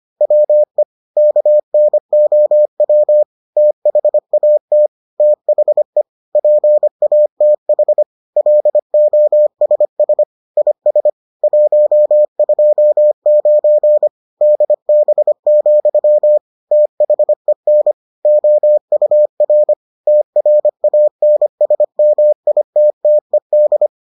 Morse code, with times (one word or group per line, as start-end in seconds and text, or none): 0.2-0.8 WE
1.2-3.2 KNOW
3.6-4.9 THAT
5.2-6.0 THE
6.3-8.0 PATH
8.4-10.2 LOSS
10.6-11.1 IS
11.4-14.1 129
14.4-16.4 DB,
16.7-17.9 THEN
18.3-19.7 OUR
20.1-23.9 TRANSMITTED